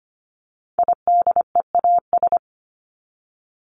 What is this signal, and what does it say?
Morse code, heard as a keyed tone, about 25 wpm, 700 Hz. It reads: IDEAS
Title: ES:S Suffix - Top 100 Words with Base Words 25wpm